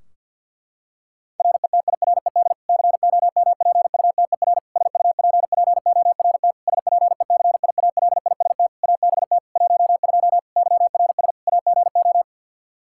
Morse code, heard as a keyed tone, about 53 wpm, 700 Hz. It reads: RETIRED COMPUTER SUPPORT SPECIALIST ABT 12 YRS AGO
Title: Practice QSO - Exchange 3 - More Info - 50wpm